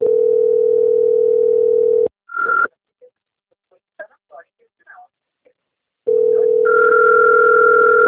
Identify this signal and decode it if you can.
Ringback tone (North America), 440 + 480 Hz, with speech.
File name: out- - - - - .24710